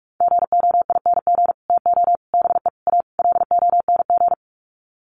Morse code, heard as a keyed tone, about 44 wpm, 700 Hz. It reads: GOING TO BE A LONG
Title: Sets of 5 Words Encore2 45wpm